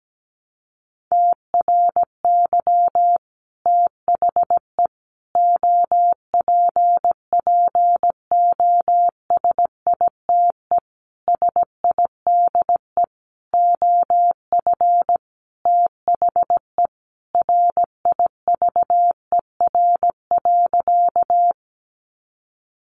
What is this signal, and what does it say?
Morse code, heard as a keyed tone, about 17 wpm, 700 Hz. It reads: TRY THE OPPOSITE SIDE OF THE RIVER.
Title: Sentences from Top 1300 Words 17wpm